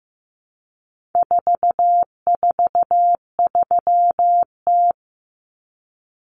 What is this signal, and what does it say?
Morse code, heard as a keyed tone, about 15 wpm, 700 Hz. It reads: 443T
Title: Common Cut Numbers - 4-Digit 15wpm